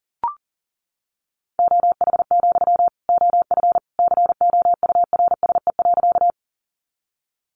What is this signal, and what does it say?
Morse code, heard as a keyed tone, about 38 wpm, 700 Hz. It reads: OH, OF COURSE.
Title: Sentences from Top 500 Words 40wpm